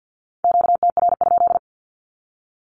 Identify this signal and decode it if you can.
Morse code, heard as a keyed tone, about 50 wpm, 700 Hz. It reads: QTR?
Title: Q-Codes with Spoken Meaning - 50wpm